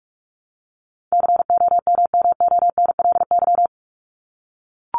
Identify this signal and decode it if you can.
Morse code, heard as a keyed tone, about 44 wpm, 700 Hz. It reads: COMMONLY
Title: LY Suffix - Top 100 Words with Base Words 45wpm